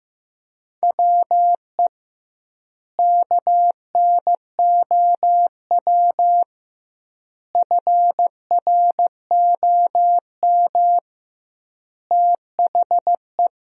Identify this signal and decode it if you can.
Morse code, heard as a keyed tone, about 15 wpm, 700 Hz. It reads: WE KNOW FROM THE
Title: 2x Word-Spacing - Sentences from Top 500 Words - 15wpm